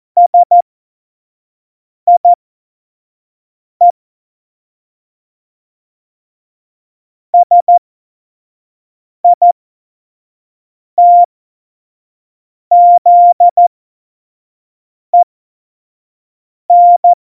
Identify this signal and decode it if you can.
Morse code, heard as a keyed tone, about 14 wpm, spaced as for 5 wpm, 700 Hz. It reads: SIE SITZEN
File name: Morsekurs / Witze / Witz 17.Dezember 2019, 25 F70